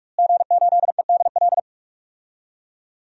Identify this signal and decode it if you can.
Morse code, heard as a keyed tone, about 46 wpm, 700 Hz. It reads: G8EDL
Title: DXCC-100 Call Signs - 5 Characters - Repeated 2x - 45wpm